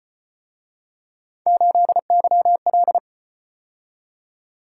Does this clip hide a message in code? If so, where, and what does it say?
Yes: Morse code from 1.5 to 3.0 s, heard as a keyed tone, about 34 wpm, 700 Hz. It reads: 8YL